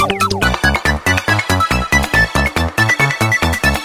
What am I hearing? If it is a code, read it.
There is no code.